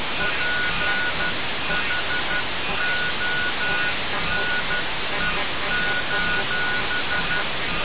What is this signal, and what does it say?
Morse code, heard as a keyed tone, about 12 wpm, spaced as for 11 wpm, 1460 Hz, with noise. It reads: PHONE:
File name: radiocap / REC z